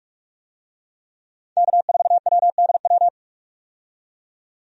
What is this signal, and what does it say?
Morse code, heard as a keyed tone, about 46 wpm, 700 Hz. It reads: K4WDW
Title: USA Call Signs - 5 Characters - 45wpm